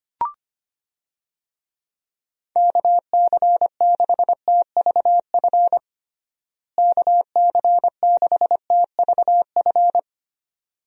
Morse code, heard as a keyed tone, about 25 wpm, 700 Hz. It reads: KC6TVF KC6TVF
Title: USA Call Signs - 6 Characters - Repeated 2x - 25wpm